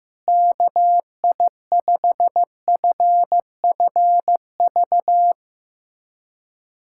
Morse code, heard as a keyed tone, about 15 wpm, 700 Hz. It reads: KI5FFV
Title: USA Callsign DE Callsign 15wpm